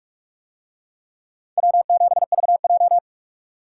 Morse code, heard as a keyed tone, about 43 wpm, 700 Hz. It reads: W7VJ